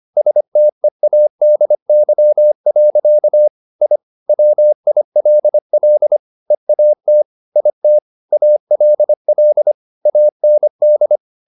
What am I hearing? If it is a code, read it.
Morse code, heard as a keyed tone, about 25 wpm, 600 Hz. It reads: STEADY. I WILL EAT IT ALL AND
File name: the old man and the sea part 1 25wpm